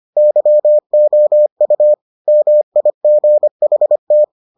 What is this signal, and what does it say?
Morse code, heard as a keyed tone, about 25 wpm, 600 Hz. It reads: YOU MIGHT